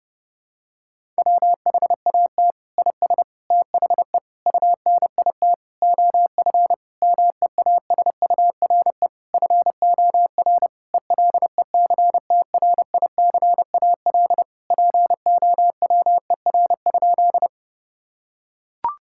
Morse code, heard as a keyed tone, about 30 wpm, 700 Hz. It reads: WHAT IS THE UNIT OF MEASURE FOR ELECTRICAL POWER?
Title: Sentences from Top 1000 Words - Review Part 6 - 30wpm